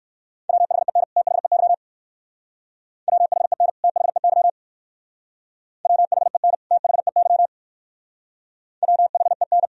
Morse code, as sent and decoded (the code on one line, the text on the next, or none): .-- .... -. / - .... -.-- / .-- .... -. / - .... -.-- / .-- .... -. / - .... -.-- / .-- .... -.
WHEN THEY WHEN THEY WHEN THEY WHEN